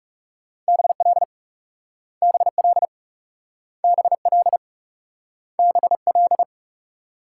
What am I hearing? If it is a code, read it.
Morse code, heard as a keyed tone, about 39 wpm, 700 Hz. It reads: BL BL BL BL